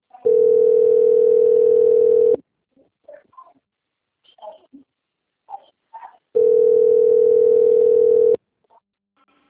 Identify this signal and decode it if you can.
Ringback tone (North America), 440 + 480 Hz.